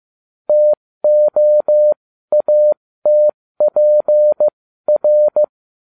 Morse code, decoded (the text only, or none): TOATPR